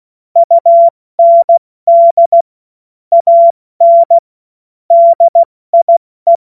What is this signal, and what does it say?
Morse code, heard as a keyed tone, about 16 wpm, 680 Hz. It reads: UND AN DIE
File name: Morsekurs / Witze / Witz 28. September 2021, 70 F80